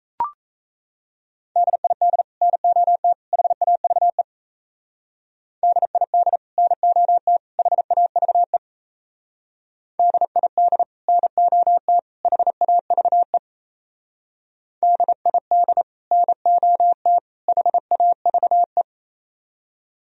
Morse code, heard as a keyed tone, about 33 wpm, 700 Hz. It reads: DID NOT HAVE DID NOT HAVE DID NOT HAVE DID NOT HAVE
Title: Speed-Racing - Sets of 3 Words 28wpm